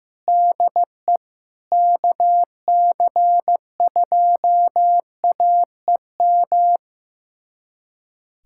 Morse code, heard as a keyed tone, about 15 wpm, 700 Hz. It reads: DE KC2AEM
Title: USA Callsign DE Callsign 15wpm